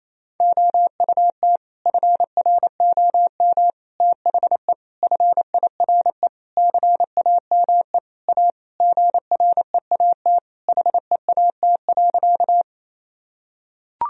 Morse code, heard as a keyed tone, about 28 wpm, 700 Hz. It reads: OUT FROM THE FIRE CAME A GREAT HEAT.
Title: Sentences from Top 1000 Words - Review Part 3 - 28wpm